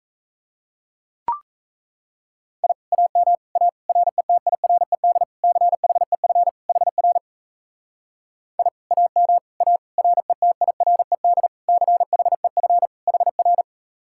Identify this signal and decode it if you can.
Morse code, heard as a keyed tone, about 36 wpm, 700 Hz. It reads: I AM A RETIRED CHEF HR I AM A RETIRED CHEF HR